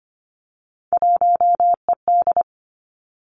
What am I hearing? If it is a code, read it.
Morse code, heard as a keyed tone, about 25 wpm, 700 Hz. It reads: 1ED
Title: Three Challenging Letter-Number Combinations 25wpm